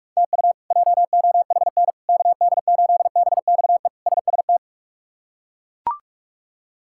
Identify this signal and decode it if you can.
Morse code, heard as a keyed tone, about 41 wpm, 700 Hz. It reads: TU JOHN KD8BXE SST
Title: K1USN SST QSO Practice - 45wpm